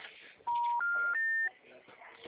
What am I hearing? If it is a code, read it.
Special information tone, with speech.